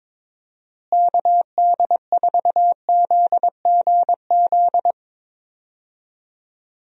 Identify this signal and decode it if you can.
Morse code, heard as a keyed tone, about 22 wpm, 700 Hz. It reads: KD4ZGZ
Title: USA Call Signs - 6 Characters - Repeated 3x - 22wpm